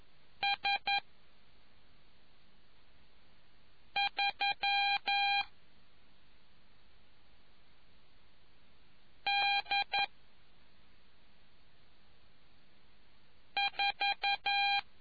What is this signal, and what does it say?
Morse code, heard as a keyed tone, about 11 wpm, 800 Hz. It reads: S 3 D 4